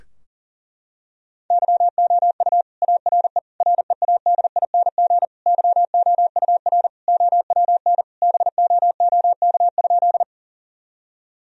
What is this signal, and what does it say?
Morse code, heard as a keyed tone, about 40 wpm, 700 Hz. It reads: YOU ARE READING YOUR OWN BOOK?
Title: Sentences from Top 1000 Words - Review Part 3 - 40wpm